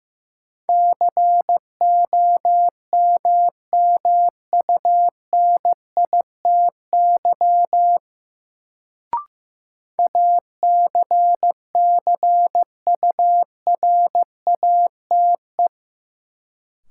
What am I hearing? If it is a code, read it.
Morse code, heard as a keyed tone, about 15 wpm, 700 Hz. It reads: COMMUNITY ACCURATE
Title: Sentences from Top 1200 Words - New Word Drill Alt - 15wpm